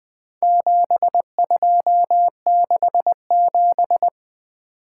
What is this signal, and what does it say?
Morse code, heard as a keyed tone, about 20 wpm, 700 Hz. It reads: 7267
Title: Four Letter-Number Combinations - 20wpm